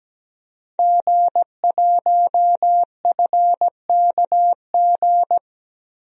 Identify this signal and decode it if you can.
Morse code, heard as a keyed tone, about 17 wpm, 700 Hz. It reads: G1FKG
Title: DXCC-100 Call Signs - 5 Characters - Repeated 2x - 17wpm